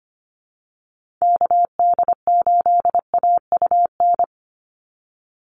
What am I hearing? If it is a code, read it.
Morse code, heard as a keyed tone, about 25 wpm, 700 Hz. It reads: KD8AUN
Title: USA Call Signs 25wpm